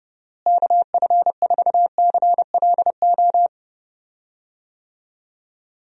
Morse code, heard as a keyed tone, about 32 wpm, 700 Hz. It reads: KF4CLO
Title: USA Call Signs - 6 Characters - Repeated 3x - 30wpm